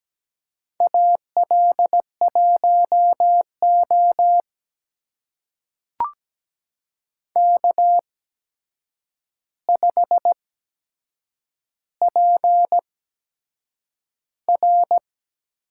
Morse code, heard as a keyed tone, about 17 wpm, spaced as for 7 wpm, 700 Hz. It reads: AL1O K5PR
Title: USA Call Signs - 4 Characters - 8x Character Spacing - 17wpm